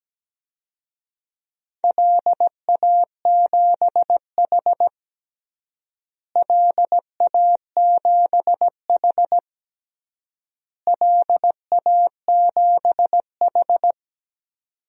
Morse code, heard as a keyed tone, about 17 wpm, 700 Hz. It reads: LA7H LA7H LA7H
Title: DXCC-100 Call Signs - 4 Characters - Repeated 3x - 17wpm